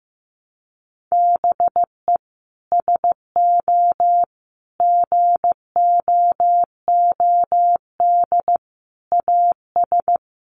Morse code, heard as a keyed tone, about 15 wpm, 700 Hz. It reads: BE SO GOOD AS
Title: Sentences from Top 900 Words 15wpm